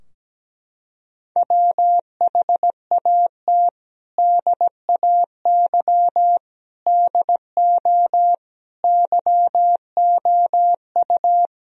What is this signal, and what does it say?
Morse code, heard as a keyed tone, about 17 wpm, 700 Hz. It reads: WHAT DAY DO YOU
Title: Sentences from Top 1000 Words - Review Part 3 - 17wpm